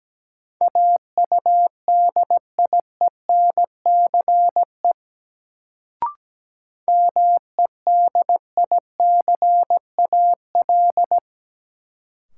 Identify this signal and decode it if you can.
Morse code, heard as a keyed tone, about 17 wpm, 700 Hz. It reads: AUDIENCE MEDICAL
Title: Sentences from Top 800 Words - New Word Drill - 17wpm